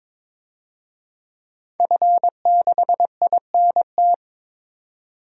Morse code, heard as a keyed tone, about 22 wpm, 700 Hz. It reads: F6INT
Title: DXCC-100 Call Signs - 5 Characters - 22wpm